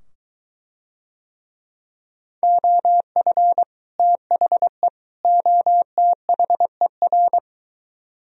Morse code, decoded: OF THE OTHER

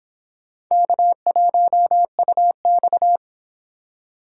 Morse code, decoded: K1UX

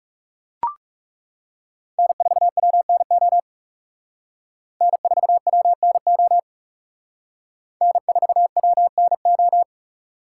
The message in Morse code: N4WNO N4WNO N4WNO

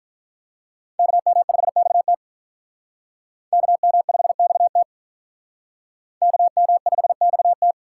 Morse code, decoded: KM5XT KM5XT KM5XT

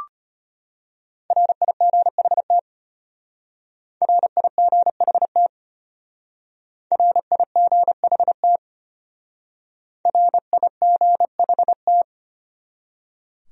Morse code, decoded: RIGHT RIGHT RIGHT RIGHT

